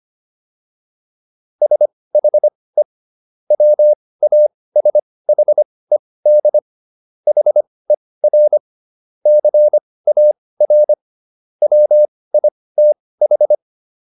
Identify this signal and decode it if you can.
Morse code, heard as a keyed tone, about 25 wpm, spaced as for 18 wpm, 600 Hz. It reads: SHE WASHED HER CAR WITH